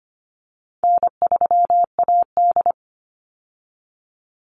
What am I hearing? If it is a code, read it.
Morse code, heard as a keyed tone, about 25 wpm, 700 Hz. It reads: N3AD
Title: Call Signs Repeated 2x - 25wpm